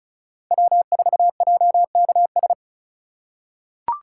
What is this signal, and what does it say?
Morse code, heard as a keyed tone, about 36 wpm, 700 Hz. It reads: W4JKS
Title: USA Call Signs - 5 Characters - 4x Character Spacing - 35wpm